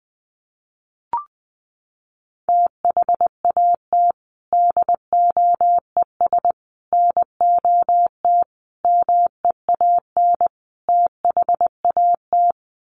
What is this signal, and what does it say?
Morse code, heard as a keyed tone, about 20 wpm, 700 Hz. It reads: THAT DOES NOT MEAN THAT
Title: Sets of 5 Words Encore 20wpm